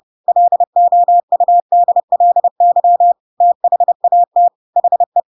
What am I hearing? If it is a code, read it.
Morse code, heard as a keyed tone, about 30 wpm, 700 Hz. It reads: LOUDLY THAT HE
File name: Pinocchio / Pinocchio ch7 30 30 ik1hge